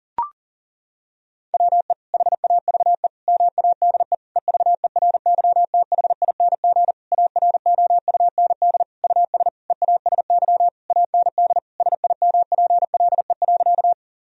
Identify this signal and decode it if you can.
Morse code, heard as a keyed tone, about 40 wpm, 700 Hz. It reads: WE HAVE MADE EVERYTHING AROUND US EASY AND SIMPLE.
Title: Sentences from Top 1000 Words - Review Part 3 - 40wpm